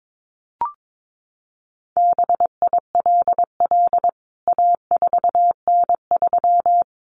Morse code, heard as a keyed tone, about 22 wpm, 700 Hz. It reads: BILL A4N3
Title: CWT Exchange 22wpm